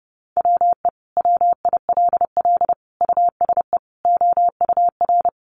Morse code, heard as a keyed tone, about 30 wpm, 700 Hz. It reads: WE WILL USE OUR
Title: Sentences from Top 1100 Words 30wpm